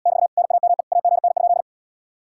Morse code, heard as a keyed tone, about 79 wpm, 700 Hz. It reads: <BT> NAME AGN?